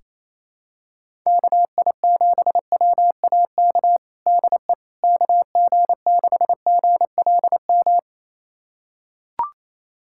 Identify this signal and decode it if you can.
Morse code, heard as a keyed tone, about 27 wpm, 700 Hz. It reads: KI7WAK DE KG6GLM
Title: USA Callsign DE Callsign - 6 Characters - 28wpm